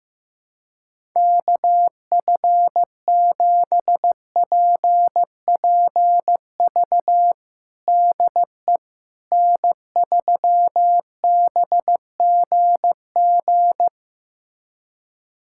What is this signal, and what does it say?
Morse code, heard as a keyed tone, about 15 wpm, 700 Hz. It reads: KF7PPV DE N3BGG
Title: USA Callsign DE Callsign 15wpm